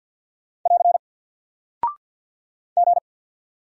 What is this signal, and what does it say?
Morse code, heard as a keyed tone, about 50 wpm, 700 Hz. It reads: <AR> G
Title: Single Character (Letter:Number:Punctuation:Prosign) Alt 50wpm